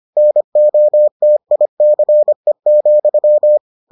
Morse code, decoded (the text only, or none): NOTICE,